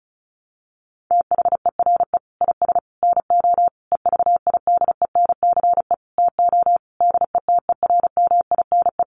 Morse code, heard as a keyed tone, about 34 wpm, 700 Hz. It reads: THERE IS NO EVIDENCE TO DETERMINE
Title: Sentences from Top 1300 Words 35wpm